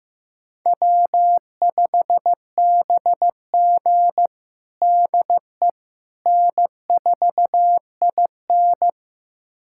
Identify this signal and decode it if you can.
Morse code, heard as a keyed tone, about 15 wpm, 700 Hz. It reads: W5BG DE N4IN
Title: USA Callsign DE Callsign - 4 Characters - 15wpm